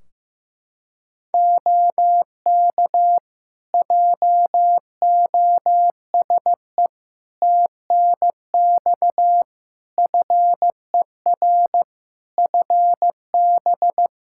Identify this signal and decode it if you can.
Morse code, heard as a keyed tone, about 15 wpm, 700 Hz. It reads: OK JOSE TNX FER FB
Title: K1USN SST QSO Practice 15wpm